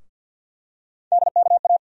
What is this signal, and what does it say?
Morse code, heard as a keyed tone, about 51 wpm, 700 Hz. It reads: DKA
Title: USA Call Sign Postfixes - 3 Letters 50wpm